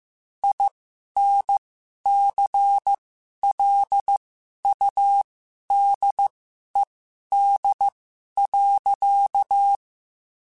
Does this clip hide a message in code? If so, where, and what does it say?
Yes: Morse code from 0.4 to 9.8 s, heard as a keyed tone, about 15 wpm, spaced as for 11 wpm, 800 Hz. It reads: INCLUDED.